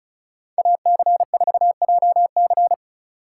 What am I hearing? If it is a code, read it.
Morse code, heard as a keyed tone, about 45 wpm, 700 Hz. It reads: AC4JC